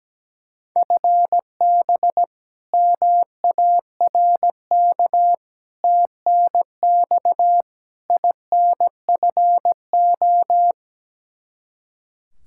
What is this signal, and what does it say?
Morse code, heard as a keyed tone, about 17 wpm, 700 Hz. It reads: FB MARK TNX INFO
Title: K1USN SST QSO Practice 17wpm